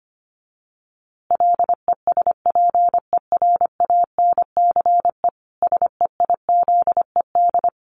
Morse code, heard as a keyed tone, about 25 wpm, 700 Hz. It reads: LESPERANCE SEIZED
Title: A Sound Of Thunder 25wpm